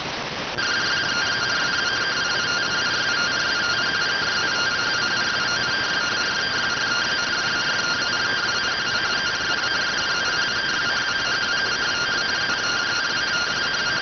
Two-tone teleprinter signal, about 170 Hz shift.